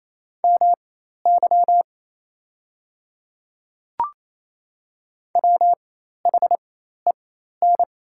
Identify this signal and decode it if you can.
Morse code, heard as a keyed tone, about 27 wpm, spaced as for 13 wpm, 700 Hz. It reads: MY WHEN
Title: 4x Character Spacing - Top 200 Words - 28wpm